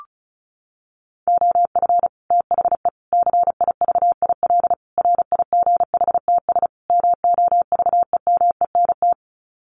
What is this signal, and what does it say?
Morse code, heard as a keyed tone, about 34 wpm, 700 Hz. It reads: OF THE CIVIL RIGHTS MOVEMENT
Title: Sets of 5 Words Encore2 35wpm